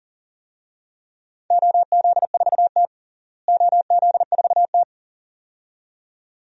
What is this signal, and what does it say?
Morse code, heard as a keyed tone, about 40 wpm, 700 Hz. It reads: OZ4T OZ4T
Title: DXCC-100 Call Signs - 4 Characters - Repeated 2x - 40wpm